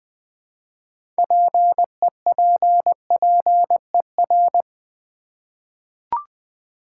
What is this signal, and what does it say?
Morse code, heard as a keyed tone, about 20 wpm, 700 Hz. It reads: PEPPER
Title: 6 Letter Words - First 3 Letters Given - 20wpm